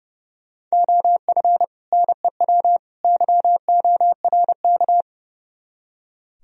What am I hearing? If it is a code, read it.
Morse code, heard as a keyed tone, about 40 wpm, 700 Hz. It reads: OF NEW YORK